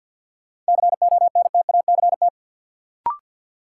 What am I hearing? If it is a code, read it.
Morse code, heard as a keyed tone, about 49 wpm, 700 Hz. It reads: CONTACT
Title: Sentences from Top 1000 Words - Review Part 7 - New Word Drill - 50wpm